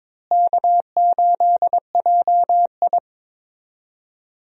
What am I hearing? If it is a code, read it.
Morse code, heard as a keyed tone, about 21 wpm, 700 Hz. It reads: K8JI
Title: USA Call Signs - 4 Characters - Repeated 3x - 22wpm